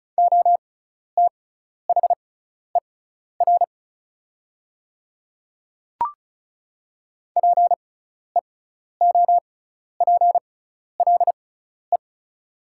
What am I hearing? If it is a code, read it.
Morse code, heard as a keyed tone, about 36 wpm, spaced as for 12 wpm, 700 Hz. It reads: OTHER PEOPLE
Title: 6x Character Spacing - Top 200 Words - 35wpm